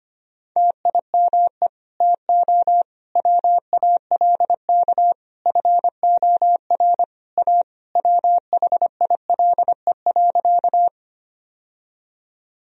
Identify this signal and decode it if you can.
Morse code, heard as a keyed tone, about 25 wpm, 700 Hz. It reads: TIME TO WALK FOR A WHILE.
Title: Sentences from Top 1000 Words - Review Part 1 - 25wpm